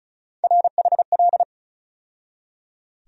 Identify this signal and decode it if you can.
Morse code, heard as a keyed tone, about 34 wpm, 700 Hz. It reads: RHL
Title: USA Call Sign Postfixes - 3 Letters 35wpm